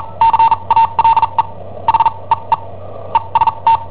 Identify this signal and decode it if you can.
Morse code, heard as a keyed tone, about 43 wpm, spaced as for 32 wpm, 910 Hz. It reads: CALE H EE EST
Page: Cassetta esercizi preparata da I7OHP per IW5ELP